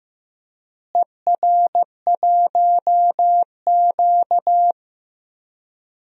Morse code, heard as a keyed tone, about 15 wpm, 700 Hz. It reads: ER1Q
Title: DXCC-100 Call Signs - 4 Characters - Repeated 3x - 15wpm